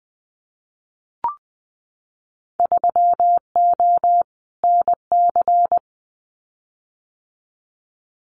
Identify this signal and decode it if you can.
Morse code, heard as a keyed tone, about 20 wpm, 700 Hz. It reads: 3O NC
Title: Winter Field Day Exchanges 20wpm